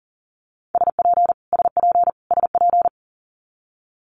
Morse code, heard as a keyed tone, about 40 wpm, 700 Hz. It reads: SP SP SP
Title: Multi-letter Phonemes Repeated 3x 40wpm